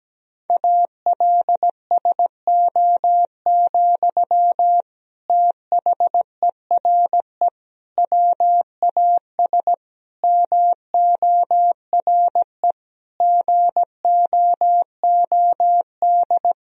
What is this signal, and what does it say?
Morse code, heard as a keyed tone, about 17 wpm, 700 Hz. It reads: ALSO, THERE WAS MORE GOOD
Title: Sentences from Top 1000 Words - Review Part 1 - 17wpm